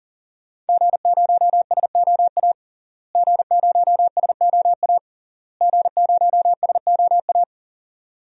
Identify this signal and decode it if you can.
Morse code, heard as a keyed tone, about 40 wpm, 700 Hz. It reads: G0SOA G0SOA G0SOA
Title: DXCC-100 Call Signs - 5 Characters - Repeated 3x - 40wpm